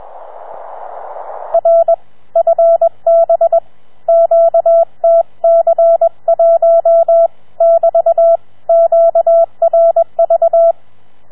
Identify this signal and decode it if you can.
Morse code, heard as a keyed tone, about 21 wpm, 670 Hz. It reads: R FB QTC1 <BT> QRV
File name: challenge-126